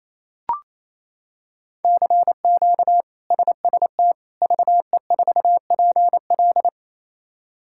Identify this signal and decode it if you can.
Morse code, heard as a keyed tone, about 28 wpm, 700 Hz. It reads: CQ SST VE4PL
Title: K1USN SST QSO Practice - 28wpm